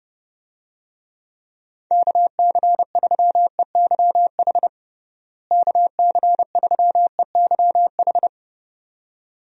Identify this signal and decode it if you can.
Morse code, heard as a keyed tone, about 30 wpm, 700 Hz. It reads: KC3EYH KC3EYH
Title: USA Call Signs - 6 Characters - Repeated 2x - 30wpm